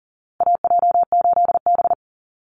Morse code, heard as a keyed tone, about 40 wpm, 700 Hz. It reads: AJ8B
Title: Call Signs Repeated 2x - 40wpm